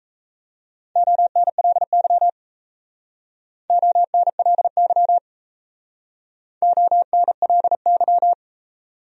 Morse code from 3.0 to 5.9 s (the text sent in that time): ONLY